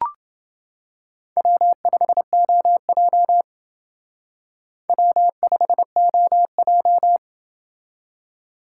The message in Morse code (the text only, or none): W5OJ W5OJ